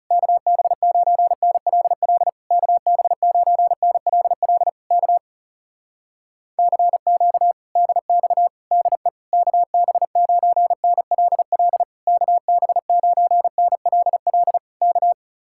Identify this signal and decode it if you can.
Morse code, heard as a keyed tone, about 40 wpm, 700 Hz. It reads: KB9NLL KB9NLL K CQ DX DE KB9NLL KB9NLL K